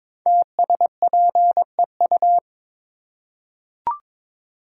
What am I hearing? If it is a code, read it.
Morse code, heard as a keyed tone, about 22 wpm, 690 Hz. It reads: TSPEU
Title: Code Groups - 5 Letter:Numbers - 22wpm